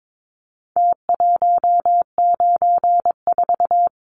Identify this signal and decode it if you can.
Morse code, heard as a keyed tone, about 22 wpm, 700 Hz. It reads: T194
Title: Numbers - Contest Style - 4-Digit 22wpm